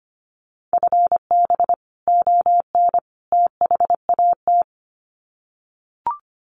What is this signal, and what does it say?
Morse code, heard as a keyed tone, about 25 wpm, 700 Hz. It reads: FB ON THAT
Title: Common QSO Phrases 25wpm